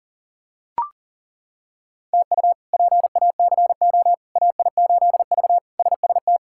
Morse code, heard as a keyed tone, about 37 wpm, 700 Hz. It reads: TU PACO AI8V SST
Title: K1USN SST QSO Practice - 40wpm